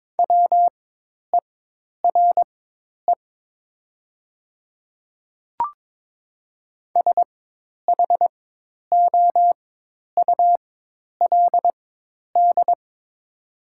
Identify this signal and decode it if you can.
Morse code, heard as a keyed tone, about 22 wpm, spaced as for 10 wpm, 700 Hz. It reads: WERE SHOULD
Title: 4x Character Spacing - Top 300 Words - 22wpm